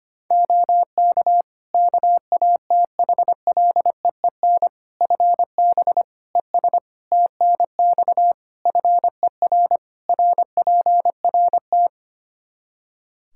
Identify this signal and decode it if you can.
Morse code, heard as a keyed tone, about 25 wpm, 700 Hz. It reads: OK KATHLEEN FB ES TNX FER RPRT
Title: Practice QSO - Exchange 2 - Info - 25wpm